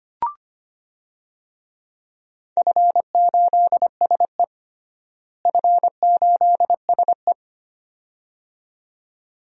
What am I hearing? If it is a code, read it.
Morse code, heard as a keyed tone, about 25 wpm, 700 Hz. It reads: F8SE F8SE
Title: DXCC-100 Call Signs - 4 Characters - Repeated 2x - 25wpm